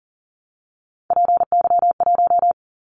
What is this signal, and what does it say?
Morse code, heard as a keyed tone, about 40 wpm, 700 Hz. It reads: PY1
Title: Callsign Prefixes for Top 100 DXCC - 40wpm